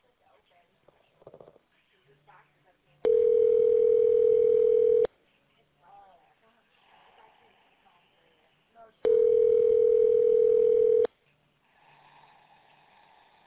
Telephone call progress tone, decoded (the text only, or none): ringback tone